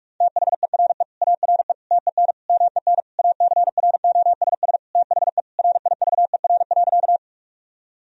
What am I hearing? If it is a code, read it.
Morse code, heard as a keyed tone, about 46 wpm, 700 Hz. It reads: THERE ARE TEN MEN ACROSS THE RIVER.
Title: Sentences from Top 1000 Words - Review Part 2 - 45wpm